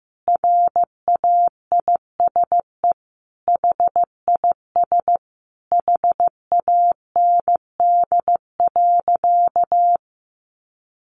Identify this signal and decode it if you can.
Morse code, heard as a keyed tone, about 15 wpm, 700 Hz. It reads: RAISE HIS HAND.
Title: Sentences from Top 1400 Words 15wpm